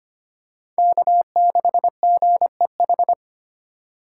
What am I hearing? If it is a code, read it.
Morse code, heard as a keyed tone, about 25 wpm, 700 Hz. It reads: K6GEH
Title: USA Call Signs - 5 Characters - 4x Character Spacing - 25wpm